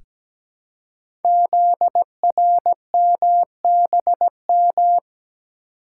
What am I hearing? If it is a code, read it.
Morse code, heard as a keyed tone, about 17 wpm, 700 Hz. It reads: ZRMBM
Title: Code Groups - 5 Letters - 17wpm